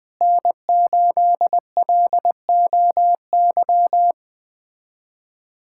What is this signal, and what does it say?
Morse code, heard as a keyed tone, about 20 wpm, 700 Hz. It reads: N8LOY
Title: USA Callsign DE Callsign - 5 Characters - 20wpm